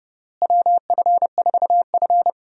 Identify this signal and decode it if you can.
Morse code, heard as a keyed tone, about 31 wpm, 700 Hz. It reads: WF4F
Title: USA Call Signs - 4 Characters - Repeated 3x - 30wpm